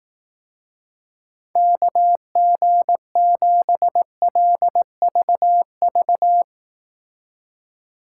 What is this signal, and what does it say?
Morse code, heard as a keyed tone, about 18 wpm, 700 Hz. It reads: KG7LVV
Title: Speed Racing - USA 2x3 Call Signs 15wpm